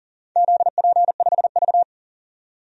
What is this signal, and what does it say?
Morse code, heard as a keyed tone, about 42 wpm, 700 Hz. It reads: ZP5V
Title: DXCC-100 Call Signs - 4 Characters - 40wpm